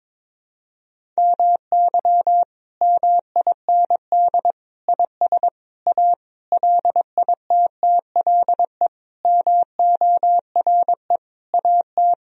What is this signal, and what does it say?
Morse code, heard as a keyed tone, about 22 wpm, 700 Hz. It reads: MY MIND IS A LITTLE MORE AT
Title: Sentences from Top 1000 Words - Review Part 1 - 22wpm